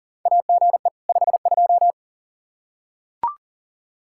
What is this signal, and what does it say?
Morse code, heard as a keyed tone, about 40 wpm, 700 Hz. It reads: AGE 52
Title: Practice QSO - Exchange 3 - More Info - 40wpm